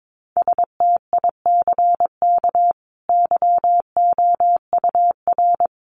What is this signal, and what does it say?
Morse code, heard as a keyed tone, about 22 wpm, 700 Hz. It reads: STICK YOUR
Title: A Sound Of Thunder 22wpm